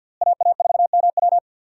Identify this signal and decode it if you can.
Morse code, heard as a keyed tone, about 52 wpm, 700 Hz. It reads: AA4MW
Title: USA Call Signs - 5 Characters - Repeated 2x - 50wpm